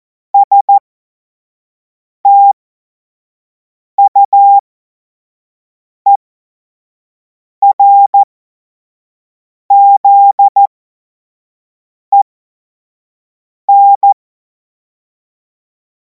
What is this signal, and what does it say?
Morse code, heard as a keyed tone, about 14 wpm, spaced as for 5 wpm, 800 Hz. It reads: STUERZEN
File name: Morsekurs / Witze / Witz 03. Februar 2026 25 F70